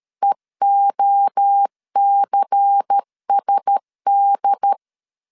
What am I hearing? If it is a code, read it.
Morse code, heard as a keyed tone, about 13 wpm, 800 Hz. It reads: EOCSD
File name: CW / DET / 1 6 1500C S A 12wpm 3440